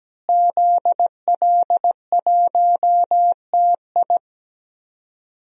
Morse code, heard as a keyed tone, about 17 wpm, 700 Hz. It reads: ZL1TI
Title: DXCC-100 Call Signs - 5 Characters - Repeated 2x - 17wpm